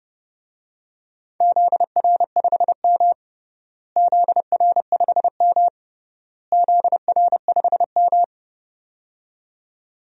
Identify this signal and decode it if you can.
Morse code, heard as a keyed tone, about 32 wpm, 700 Hz. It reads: ZR5M ZR5M ZR5M
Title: DXCC-100 Call Signs - 4 Characters - Repeated 3x - 30wpm